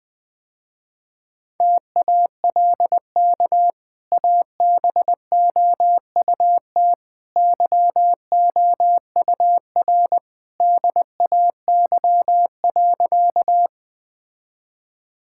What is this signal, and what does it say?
Morse code, heard as a keyed tone, about 21 wpm, 700 Hz. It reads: TALK ABOUT YOUR DAY.
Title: Easy Sentences with Hints - 20wpm